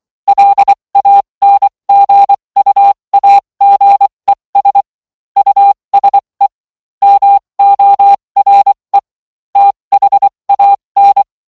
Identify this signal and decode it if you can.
Morse code, heard as a keyed tone, about 24 wpm, spaced as for 20 wpm, 800 Hz. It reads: LANGUAGES USE MORE THAN